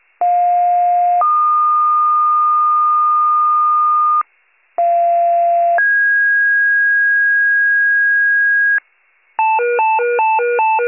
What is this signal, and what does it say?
Two-tone page, heard as a one-second tone, then a three-second tone, about 710 then 1220 Hz, about 690 then 1650 Hz, with noise. It is heard from 0.2 to 8.8 s.